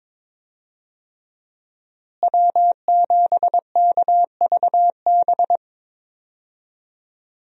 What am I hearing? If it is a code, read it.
Morse code, heard as a keyed tone, about 22 wpm, 700 Hz. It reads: W7KVB
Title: USA Call Signs - 5 Characters - Repeated 3x - 22wpm